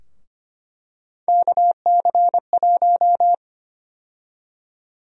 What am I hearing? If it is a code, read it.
Morse code, heard as a keyed tone, about 25 wpm, 700 Hz. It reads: KC1
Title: USA Call Sign Prefixes - Repeated 2x - 25wpm